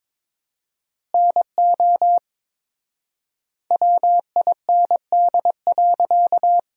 Morse code, heard as a keyed tone, about 21 wpm, 700 Hz. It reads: NO WIND.
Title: 4x Word-Spacing - Sentences from Top 400 Words - 22wpm